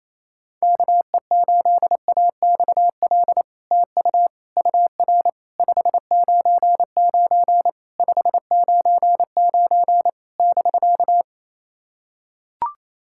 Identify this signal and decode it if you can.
Morse code, heard as a keyed tone, about 28 wpm, 700 Hz. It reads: KE8AXL TU UR 599 599 <BK>
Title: SOTA QSO Practice 28wpm